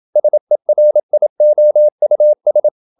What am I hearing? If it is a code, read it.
Morse code, heard as a keyed tone, about 27 wpm, 600 Hz. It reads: SERIOUS
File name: Never 0027wpm